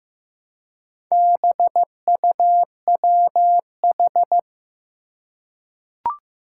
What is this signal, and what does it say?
Morse code, heard as a keyed tone, about 15 wpm, 700 Hz. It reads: BUWH
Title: Four Challenging Letter-Number Combinations 15wpm